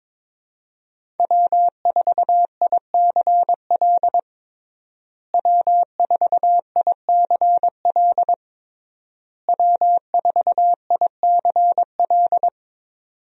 Morse code, heard as a keyed tone, about 22 wpm, 700 Hz. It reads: W4ICL W4ICL W4ICL
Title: USA Call Signs - 5 Characters - Repeated 3x - 22wpm